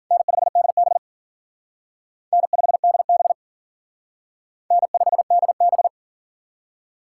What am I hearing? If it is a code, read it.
Morse code, heard as a keyed tone, about 47 wpm, 700 Hz. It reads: N5DB N5DB N5DB
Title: Speed Racing - USA 1x2 Call Signs 40wpm